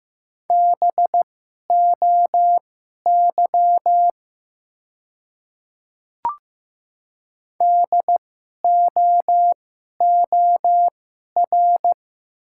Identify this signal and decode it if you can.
Morse code, heard as a keyed tone, about 15 wpm, spaced as for 11 wpm, 700 Hz. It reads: BOY DOOR
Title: 2x Character Spacing - Top 300 Words - 15wpm